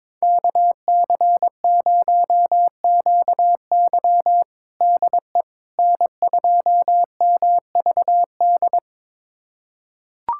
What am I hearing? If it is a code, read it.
Morse code, heard as a keyed tone, about 22 wpm, 700 Hz. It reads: KC0QY DE N2MVD
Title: USA Callsign DE Callsign - 5 Characters - 22wpm